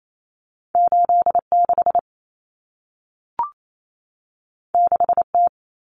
Morse code, heard as a keyed tone, about 28 wpm, 700 Hz. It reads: 86 6T
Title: Common Cut Numbers - 2-Digit 28wpm